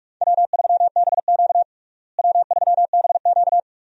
Morse code, heard as a keyed tone, about 46 wpm, 700 Hz. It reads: W3BQ W3BQ
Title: USA Call Signs - 4 Characters - Repeated 2x - 45wpm